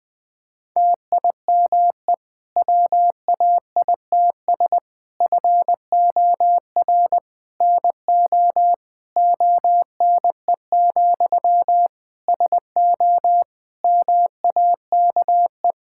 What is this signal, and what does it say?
Morse code, heard as a keyed tone, about 20 wpm, 700 Hz. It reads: TIME WAITS FOR NO ONE, SO MAKE
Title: Difficult Sentences with Hints - 20wpm